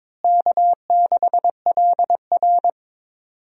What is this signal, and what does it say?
Morse code, heard as a keyed tone, about 22 wpm, 700 Hz. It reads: K6LR
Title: USA Call Signs - 4 Characters - Repeated 3x - 22wpm